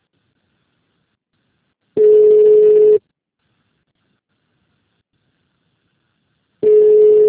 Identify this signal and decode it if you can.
Ringback tone (Europe, 425 Hz).
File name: out- -2026- - - .74981